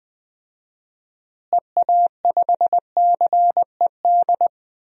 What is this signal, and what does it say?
Morse code, heard as a keyed tone, about 20 wpm, 700 Hz. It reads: EA5CED
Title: DXCC-100 Call Signs - 6 Characters - Repeated 3x - 20wpm